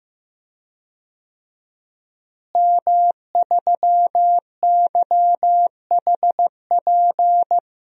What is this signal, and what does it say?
Morse code, heard as a keyed tone, about 15 wpm, 700 Hz. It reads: M3YHP
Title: DXCC-100 Call Signs - 5 Characters - Repeated 2x - 15wpm